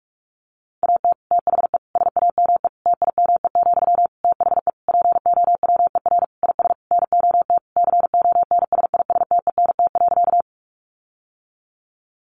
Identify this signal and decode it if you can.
Morse code, heard as a keyed tone, about 44 wpm, 700 Hz. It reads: AT THE SAME TIME, THE POWER IS NOT CONSISTENT.
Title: Sentences from Top 1500 Words 45wpm